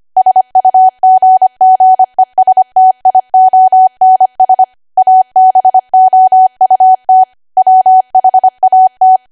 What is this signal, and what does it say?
Morse code, heard as a keyed tone, about 25 wpm, 750 Hz. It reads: SUGGESTIONS ABOUT WHAT
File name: cwa / W1AW-Level3 / 25WPM